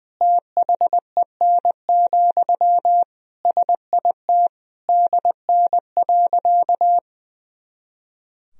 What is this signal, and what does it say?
Morse code, heard as a keyed tone, about 21 wpm, 700 Hz. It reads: THEN, SIT DN.
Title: Sentences from Top 500 Words and Abbreviations 20wpm